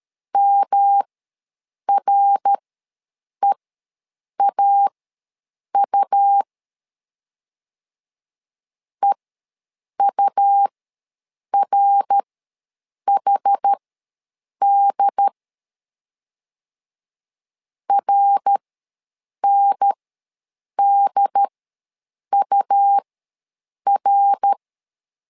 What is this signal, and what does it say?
Morse code, heard as a keyed tone, about 13 wpm, spaced as for 7 wpm, 800 Hz. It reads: MREAU EURHD RNDUR
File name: LEZIONI / 3 MWRO / 3 MWR ws 12wpm 1809